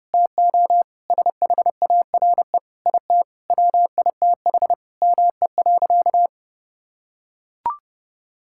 Morse code, heard as a keyed tone, about 30 wpm, 700 Hz. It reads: TO SHARE IT WITH ME.
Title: Sentences from Top 1000 Words - Review Part 6 - 30wpm